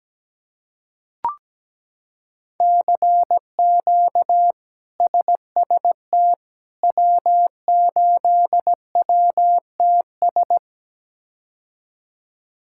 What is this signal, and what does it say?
Morse code, heard as a keyed tone, about 17 wpm, 700 Hz. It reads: CQ SST W8WTS
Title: K1USN SST QSO Practice - 17wpm